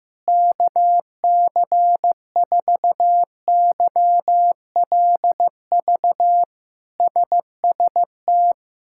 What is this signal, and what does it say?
Morse code, heard as a keyed tone, about 15 wpm, 700 Hz. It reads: KC4YLV SST
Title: K1USN SST QSO Practice - 15wpm